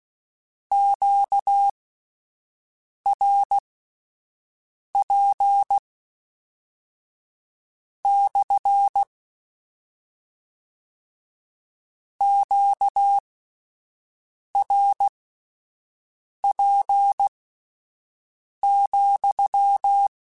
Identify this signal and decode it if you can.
Morse code, heard as a keyed tone, about 16 wpm, spaced as for 5 wpm, 800 Hz. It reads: QRP / QRP,
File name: Q Calls / Q Calls-mp3s / Q CALLS SPSL @10